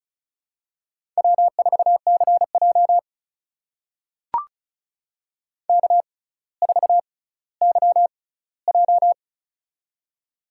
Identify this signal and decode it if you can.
Morse code, heard as a keyed tone, about 34 wpm, spaced as for 18 wpm, 700 Hz. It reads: W4CJ K4YJ
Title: USA Call Signs - 4 Characters - 6x Character Spacing - 35wpm